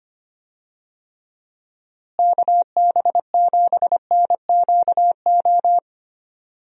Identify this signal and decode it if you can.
Morse code, heard as a keyed tone, about 25 wpm, 700 Hz. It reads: KB7NQO
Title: USA Call Signs - 6 Characters - Repeated 3x - 25wpm